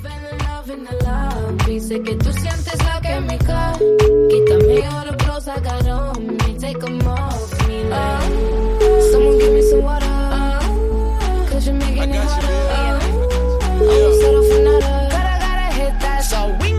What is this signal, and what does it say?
Ringback tone (Europe, 425 Hz), with noise.